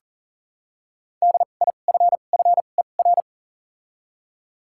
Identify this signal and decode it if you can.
Morse code, heard as a keyed tone, about 40 wpm, spaced as for 29 wpm, 700 Hz. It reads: DIFFER
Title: 2x Character Spacing - Top 500 Words - 40wpm